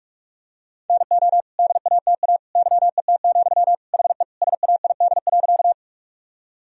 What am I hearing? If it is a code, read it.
Morse code, heard as a keyed tone, about 46 wpm, 700 Hz. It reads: NO DATA YET, HE SAID.
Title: Sentences from Top 1000 Words - Review Part 3 - 45wpm